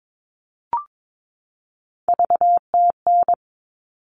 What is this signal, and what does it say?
Morse code, heard as a keyed tone, about 22 wpm, 700 Hz. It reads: VTN
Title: All Cut Numbers - 3-Digit 22wpm